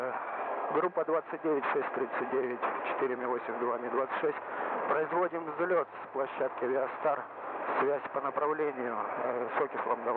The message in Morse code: none

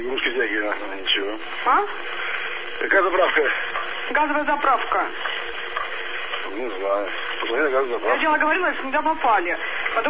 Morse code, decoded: none

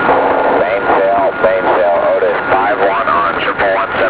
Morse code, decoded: none

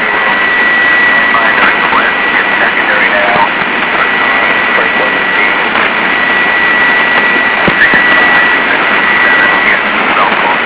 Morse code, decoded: none

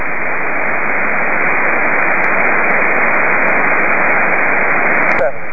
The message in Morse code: none

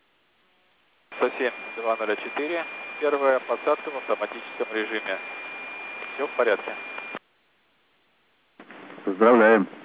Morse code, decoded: none

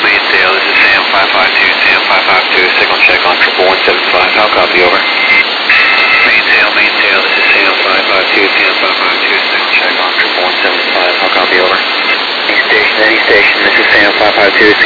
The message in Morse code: none

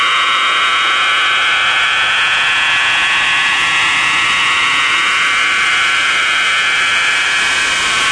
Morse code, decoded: none

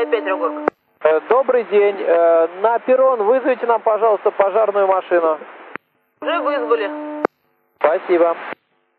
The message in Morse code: none

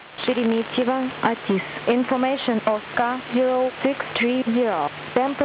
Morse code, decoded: none

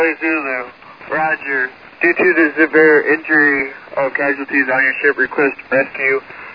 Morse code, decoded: none

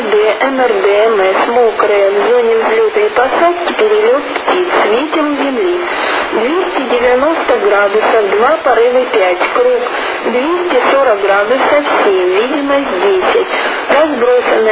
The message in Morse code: none

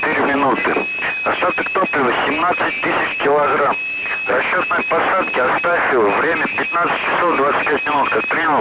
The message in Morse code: none

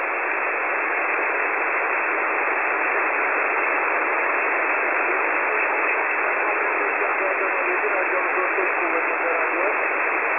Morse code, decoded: none

